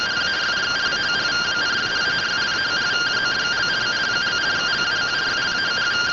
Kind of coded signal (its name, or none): two-tone teleprinter signal